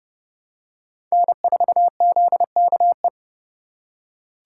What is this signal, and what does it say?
Morse code, heard as a keyed tone, about 31 wpm, 700 Hz. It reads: N4ZKE
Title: USA Call Signs - 5 Characters - 4x Character Spacing - 30wpm